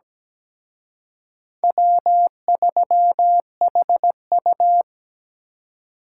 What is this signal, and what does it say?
Morse code, heard as a keyed tone, about 20 wpm, 700 Hz. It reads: W3HU
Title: Speed Racing - USA 1x2 Call Signs 17wpm